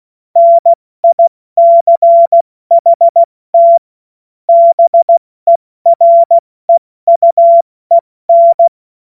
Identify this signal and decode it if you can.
Morse code, heard as a keyed tone, about 16 wpm, 680 Hz. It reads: NICHT BEREUEN
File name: Morsekurs / Witze / Witz 08. November 2022 70 F80